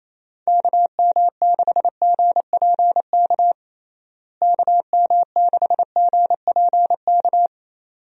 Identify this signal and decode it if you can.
Morse code, heard as a keyed tone, about 28 wpm, 700 Hz. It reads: KM6GPK KM6GPK
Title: USA Call Signs - 6 Characters - Repeated 2x - 28wpm